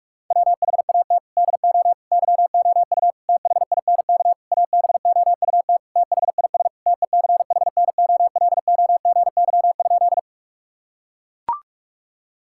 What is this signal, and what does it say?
Morse code, heard as a keyed tone, about 41 wpm, 700 Hz. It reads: WHAT DO YOU THINK ABOUT THIS TECHNOLOGY?
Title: Sentences from Top 1000 Words - Review Part 5 - 45wpm